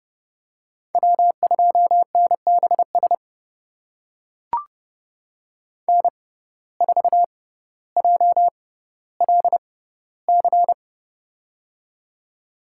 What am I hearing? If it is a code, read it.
Morse code, heard as a keyed tone, about 30 wpm, spaced as for 15 wpm, 700 Hz. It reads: W2NBS N4JLC
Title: USA Call Signs - 6x Character Spacing - 30wpm